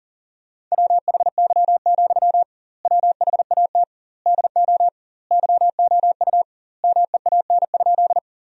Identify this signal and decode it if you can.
Morse code, heard as a keyed tone, about 40 wpm, 700 Hz. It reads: WHY, WHAT DO YOU MEAN?
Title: 2x Word-Spacing - Sentences from Top 200 Words - 40wpm